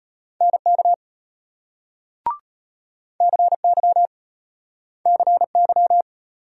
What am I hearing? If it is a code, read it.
Morse code, heard as a keyed tone, about 33 wpm, 700 Hz. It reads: NK CY CY